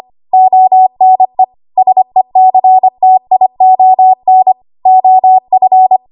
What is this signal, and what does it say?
Morse code, heard as a keyed tone, about 25 wpm, 750 Hz. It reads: ONE SECTION OF
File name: Archive / 25 WPM / 25WPM